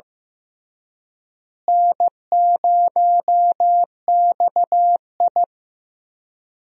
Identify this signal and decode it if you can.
Morse code, heard as a keyed tone, about 15 wpm, 700 Hz. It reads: N0XI
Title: USA Call Signs - 4 Characters - Repeated 3x - 15wpm